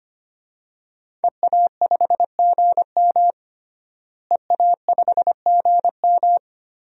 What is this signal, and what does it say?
Morse code, heard as a keyed tone, about 25 wpm, 700 Hz. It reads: EA5GM EA5GM
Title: DXCC-100 Call Signs - 5 Characters - Repeated 2x - 25wpm